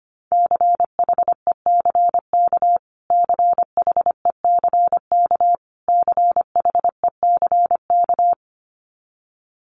Morse code, heard as a keyed tone, about 25 wpm, 700 Hz. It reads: CHECK CHECK CHECK
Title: Top 500 Words Repeated 25wpm